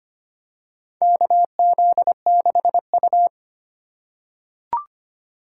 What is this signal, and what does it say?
Morse code, heard as a keyed tone, about 25 wpm, 700 Hz. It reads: KZ6U
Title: USA Call Signs - 4 Characters - 3x Character Spacing - 25wpm